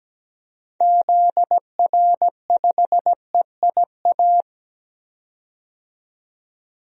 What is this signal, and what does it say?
Morse code, heard as a keyed tone, about 17 wpm, 700 Hz. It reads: ZR5EIA
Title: DXCC-100 Call Signs - 6 Characters - 17wpm